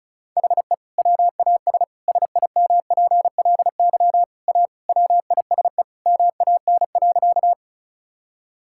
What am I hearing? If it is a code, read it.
Morse code, heard as a keyed tone, about 34 wpm, 700 Hz. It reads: HE WAS SIMPLY A WISE MAN.
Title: Sentences from Top 1600 Words 35wpm